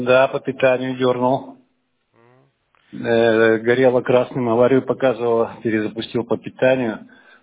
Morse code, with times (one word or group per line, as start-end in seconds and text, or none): none